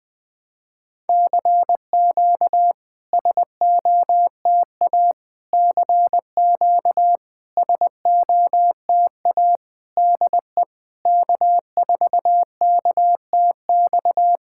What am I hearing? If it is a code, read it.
Morse code, heard as a keyed tone, about 21 wpm, 700 Hz. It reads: CQ SOTA CQ SOTA DE K4KTX
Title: SOTA QSO Practice 20wpm